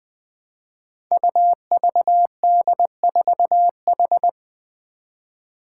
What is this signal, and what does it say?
Morse code, heard as a keyed tone, about 21 wpm, 700 Hz. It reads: UVD4H
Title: Five Challenging Letter-Number Combinations 20wpm